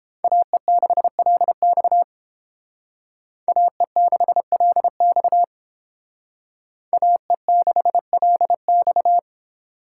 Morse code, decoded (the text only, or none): AE6LX AE6LX AE6LX